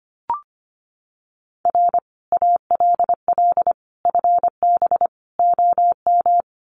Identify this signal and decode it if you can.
Morse code, heard as a keyed tone, about 25 wpm, 700 Hz. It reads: R ALL FB OM
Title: Common QSO Phrases 25wpm